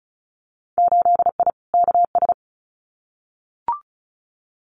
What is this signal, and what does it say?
Morse code, heard as a keyed tone, about 34 wpm, 700 Hz. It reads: 8I KS
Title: Winter Field Day Exchanges Encore 35wpm